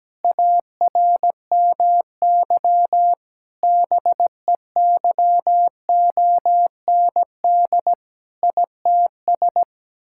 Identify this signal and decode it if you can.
Morse code, heard as a keyed tone, about 17 wpm, 700 Hz. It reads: ARMY BEYOND ITS
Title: Sentences from Top 1000 Words - Review Part 6 - 17wpm